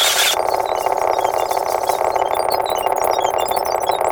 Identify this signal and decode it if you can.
There is no code.